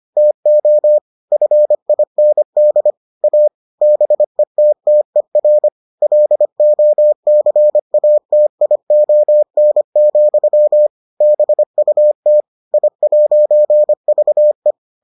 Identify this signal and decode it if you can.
Morse code, heard as a keyed tone, about 25 wpm, 600 Hz. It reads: TO FIND A BETTER LOCATION, BUT I'VE